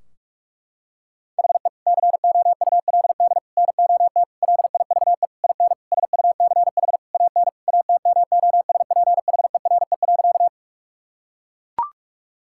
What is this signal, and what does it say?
Morse code, heard as a keyed tone, about 41 wpm, 700 Hz. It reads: HE COULD NOT LIVE IN SUCH AN ATMOSPHERE.
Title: Sentences from Top 1600 Words 45wpm